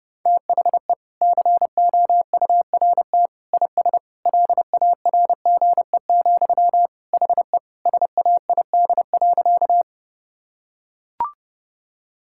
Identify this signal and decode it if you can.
Morse code, heard as a keyed tone, about 30 wpm, 700 Hz. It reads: THE COURT IS LARGE, HE SAID.
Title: Sentences from Top 1600 Words 30wpm